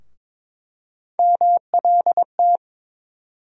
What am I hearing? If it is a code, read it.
Morse code, heard as a keyed tone, about 21 wpm, 700 Hz. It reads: MLT
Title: USA Call Sign Postfixes - 3 Letters 22wpm